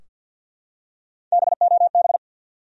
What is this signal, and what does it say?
Morse code, heard as a keyed tone, about 46 wpm, 700 Hz. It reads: BOB
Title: USA Common Names - Max 3 Letters - 50wpm